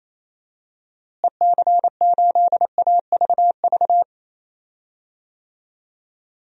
Morse code, heard as a keyed tone, about 28 wpm, 700 Hz. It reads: EC8AVV
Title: DXCC-100 Call Signs - 6 Characters - Repeated 3x - 28wpm